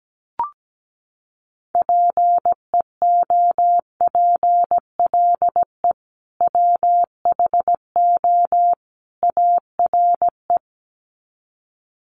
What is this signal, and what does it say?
Morse code, heard as a keyed tone, about 17 wpm, 700 Hz. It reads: PEOPLE WHO ARE
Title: Sets of 3 Words 17wpm